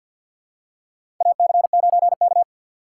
Morse code, heard as a keyed tone, about 47 wpm, 700 Hz. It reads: AC9K
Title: USA Call Signs - 4 Characters - 3x Character Spacing - 50wpm